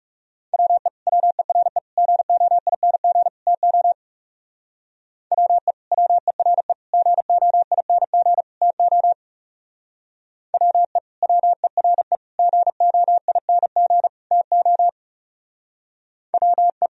Morse code, heard as a keyed tone, about 36 wpm, 700 Hz. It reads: WE WERE GOING TO WE WERE GOING TO WE WERE GOING TO WE
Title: Speed-Racing - Sets of 4 Words 30wpm